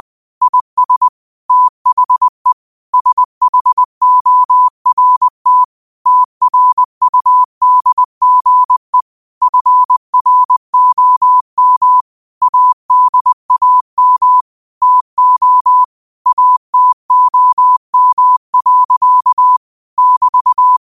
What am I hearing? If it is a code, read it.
Morse code, heard as a keyed tone, about 20 wpm, 1000 Hz. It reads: IS THE SHORT TRUDGE FROM ADAM TO ATOM. <BT>
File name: qotdfiles / 20wpm